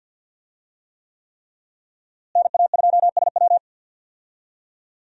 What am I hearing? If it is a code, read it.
Morse code, heard as a keyed tone, about 52 wpm, 700 Hz. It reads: NA2SW
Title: USA Call Signs - 5 Characters - Repeated 3x - 50wpm